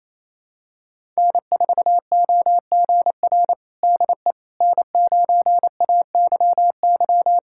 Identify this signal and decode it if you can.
Morse code, heard as a keyed tone, about 28 wpm, 700 Hz. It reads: N4OGR DE N9AYY
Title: USA Callsign DE Callsign - 5 Characters - 28wpm